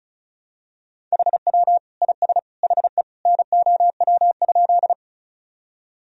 Morse code, heard as a keyed tone, about 34 wpm, 700 Hz. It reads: HW IS HE NOW?
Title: Sentences from Top 100 Words and Abbreviations 35wpm